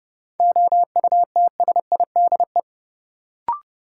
Morse code, heard as a keyed tone, about 30 wpm, 700 Hz. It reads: OUTSIDE
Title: Sentences from Top 900 Words - New Word Drill - 30wpm